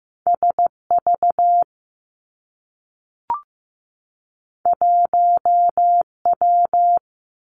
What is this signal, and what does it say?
Morse code, heard as a keyed tone, about 15 wpm, 700 Hz. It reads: S V 1 W
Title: Two Challenging Letter-Number Combinations 15wpm